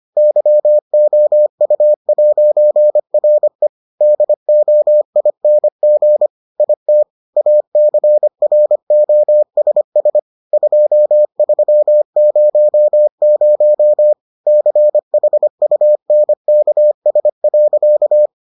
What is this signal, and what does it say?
Morse code, heard as a keyed tone, about 25 wpm, 600 Hz. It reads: YOU'RE DOING IT ACROSS 2300 CHUNKS.